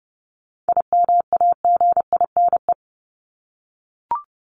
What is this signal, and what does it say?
Morse code, heard as a keyed tone, about 29 wpm, 700 Hz. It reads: IMAGINE
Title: Top 1251-1500 Words 30wpm